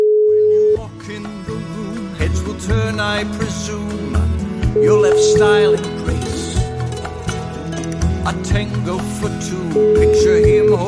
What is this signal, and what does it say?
Ringback tone (Europe, 425 Hz), with speech.